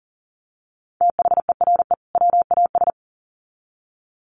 Morse code, heard as a keyed tone, about 40 wpm, 700 Hz. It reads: THERE WAS
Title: Sets of 2 Words 40wpm